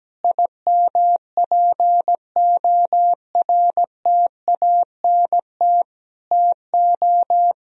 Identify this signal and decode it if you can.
Morse code, heard as a keyed tone, about 17 wpm, 700 Hz. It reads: IMPORTANT TO
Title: Sentences from Top 1000 Words - Review Part 4 - 17wpm